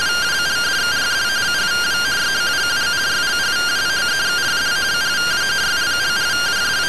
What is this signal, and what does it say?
Two-tone teleprinter signal, about 170 Hz shift.
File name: findings / CT AXIAL -111.gen-06 rtty rx